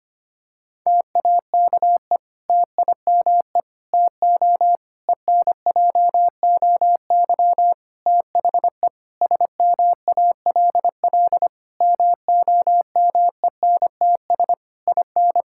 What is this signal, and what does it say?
Morse code, heard as a keyed tone, about 25 wpm, 700 Hz. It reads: TAKE TIME TO ENJOY THE SMALL MOMENTS IN
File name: practice / 25wpm / Difficult Sentences with Hints - 25wpm